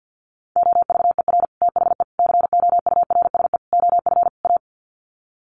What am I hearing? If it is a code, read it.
Morse code, heard as a keyed tone, about 50 wpm, 700 Hz. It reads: OVER THE COURSE OF A
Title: Sets of 5 Words Encore 50wpm